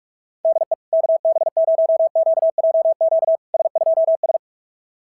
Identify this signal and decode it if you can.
Morse code, heard as a keyed tone, about 48 wpm, 690 Hz. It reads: DE KB0QJQ S2S
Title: SOTA QSO Practice 45wpm